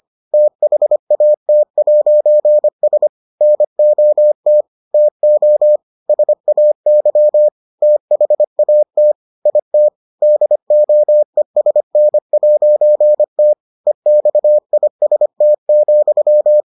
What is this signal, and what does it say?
Morse code, heard as a keyed tone, about 25 wpm, 600 Hz. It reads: THAT'S NOT TO SAY THAT IT DOESN'T EXIST,